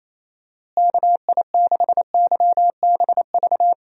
Morse code, heard as a keyed tone, about 28 wpm, 700 Hz. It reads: KI6YBV
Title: USA Call Signs - 6 Characters - Repeated 3x - 28wpm